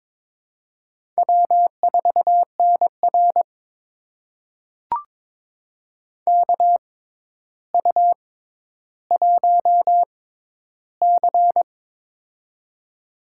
Morse code, heard as a keyed tone, about 22 wpm, spaced as for 11 wpm, 700 Hz. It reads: W4NR KU1C
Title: USA Call Signs - 4 Characters - 6x Character Spacing - 22wpm